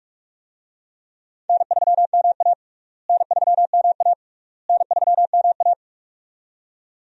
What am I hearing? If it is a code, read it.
Morse code, heard as a keyed tone, about 46 wpm, 700 Hz. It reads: N3MA N3MA N3MA
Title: USA Call Signs - 4 Characters - Repeated 3x - 45wpm